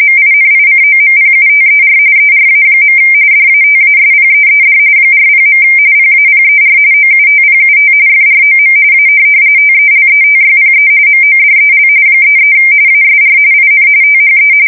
RTTY, about 45.45 baud. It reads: MINICOMPUTER ERA. DATASPEED 40. PAPER OUTPUT FROM A TELETYPE MODEL 33 ASR IN THE
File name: fsk170 / teletype-corp